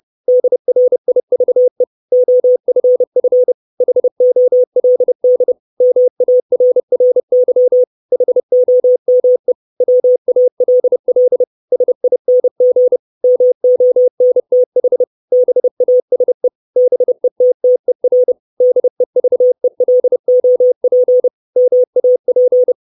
Morse code, heard as a keyed tone, about 30 wpm, 500 Hz. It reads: DRIVE OFF HOLD MARRY HOME WALL SING MONTH BASE BETTER DEVELOP MAP
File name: 632 Words 30WPM